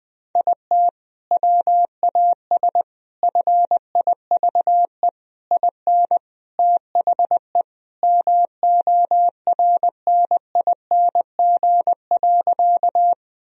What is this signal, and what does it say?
Morse code, heard as a keyed tone, about 21 wpm, 700 Hz. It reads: IT WAS FIVE IN THE MORNING.
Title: Sentences from Top 1000 Words - Review Part 2 - 20wpm